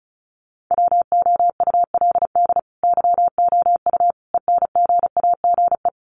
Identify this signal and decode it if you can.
Morse code, heard as a keyed tone, about 34 wpm, 700 Hz. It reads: WOULD YOU ENGAGE
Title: Sentences from Top 1500 Words 35wpm